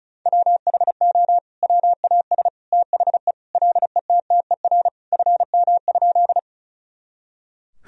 Morse code, heard as a keyed tone, about 36 wpm, 700 Hz. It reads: WHO WAS THE LETTER FM?
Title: Sentences from Top 300 Words and Abbreviations 35wpm